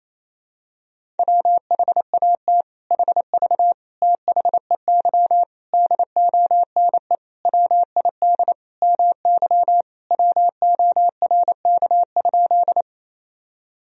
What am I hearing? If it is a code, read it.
Morse code, heard as a keyed tone, about 28 wpm, 700 Hz. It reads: WHAT HV THEY DONE WID MY WORK?
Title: Sentences from Top 500 Words and Abbreviations 28wpm